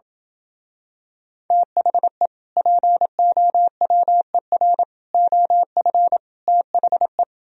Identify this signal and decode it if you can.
Morse code, heard as a keyed tone, about 27 wpm, 700 Hz. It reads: THE POWER OF THE